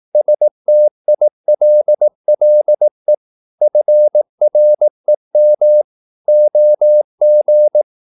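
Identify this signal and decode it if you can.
Morse code, heard as a keyed tone, about 18 wpm, 600 Hz. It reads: STILLE FREM OG
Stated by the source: Slow (QRS) Morse practice at 18 wpm, CW med Gnister.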